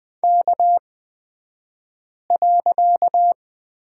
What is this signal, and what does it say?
Morse code, heard as a keyed tone, about 20 wpm, spaced as for 5 wpm, 700 Hz. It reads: K.
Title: 254 Lesson 41d - Period - Two Characters 20wpm